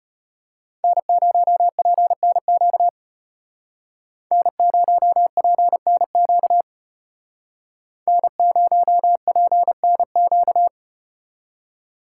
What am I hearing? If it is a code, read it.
Morse code, heard as a keyed tone, about 36 wpm, 700 Hz. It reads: N0PNQ N0PNQ N0PNQ